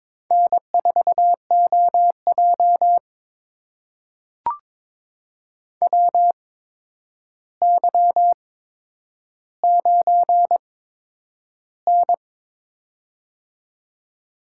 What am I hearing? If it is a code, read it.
Morse code, heard as a keyed tone, about 21 wpm, spaced as for 9 wpm, 700 Hz. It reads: N4OJ WY9N
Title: USA Call Signs - 4 Characters - 8x Character Spacing - 22wpm